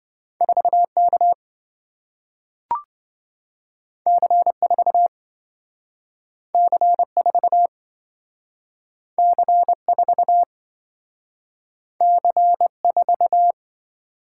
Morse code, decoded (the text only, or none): 4K C4 C4 C4 C4